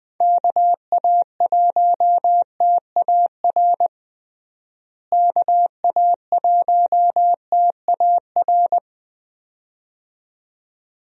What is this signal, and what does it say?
Morse code, heard as a keyed tone, about 21 wpm, 700 Hz. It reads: KA1TAR KA1TAR
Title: USA Call Signs - 6 Characters - Repeated 2x - 20wpm